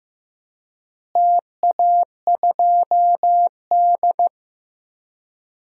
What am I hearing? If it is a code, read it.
Morse code, heard as a keyed tone, about 15 wpm, 700 Hz. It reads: TA2D
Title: DXCC-100 Call Signs - 4 Characters - Repeated 3x - 15wpm